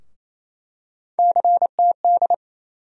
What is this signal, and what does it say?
Morse code, heard as a keyed tone, about 28 wpm, 700 Hz. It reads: CTD
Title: USA Call Sign Postfixes - 3 Letters 28wpm